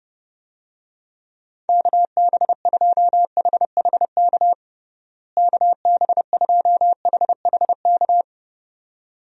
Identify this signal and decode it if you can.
Morse code, heard as a keyed tone, about 30 wpm, 700 Hz. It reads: KB2HHK KB2HHK
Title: USA Call Signs - 6 Characters - Repeated 2x - 30wpm